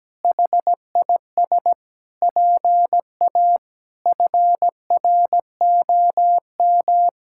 Morse code, heard as a keyed tone, about 17 wpm, 700 Hz. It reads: HIS PA FROM
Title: Pangram Challenge v2 - 17wpm